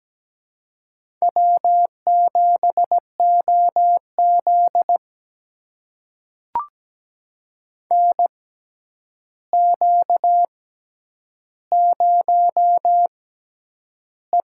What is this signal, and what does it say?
Morse code, heard as a keyed tone, about 17 wpm, spaced as for 9 wpm, 700 Hz. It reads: W7OZ NQ0E
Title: USA Call Signs - 4 Characters - 6x Character Spacing - 17wpm